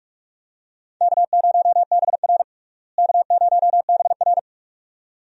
Morse code, heard as a keyed tone, about 41 wpm, 700 Hz. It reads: K0BR K0BR
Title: USA Call Signs - 4 Characters - Repeated 2x - 45wpm